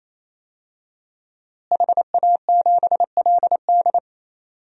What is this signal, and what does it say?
Morse code, heard as a keyed tone, about 27 wpm, 700 Hz. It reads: HA7LD
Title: DXCC-100 Call Signs - 5 Characters - Repeated 3x - 28wpm